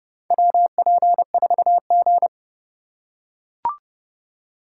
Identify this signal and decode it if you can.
Morse code, heard as a keyed tone, about 30 wpm, 700 Hz. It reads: WP4G
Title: USA Call Signs - 4 Characters - 3x Character Spacing - 30wpm